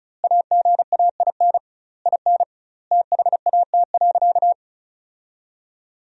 Morse code, heard as a keyed tone, about 36 wpm, 700 Hz. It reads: AGAIN IN THAT.
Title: 2x Word-Spacing - Sentences from Top 200 Words - 35wpm